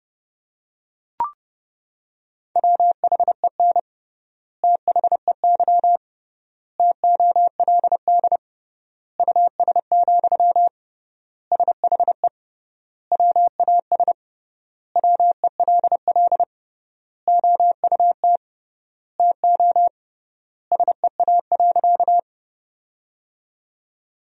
Morse code, heard as a keyed tone, about 30 wpm, 700 Hz. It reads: WHEN THEY TOLD US, SHE WAS WELL OUT TO SEA.
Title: 3x Word-Spacing - Sentences from Top 400 Words - 30wpm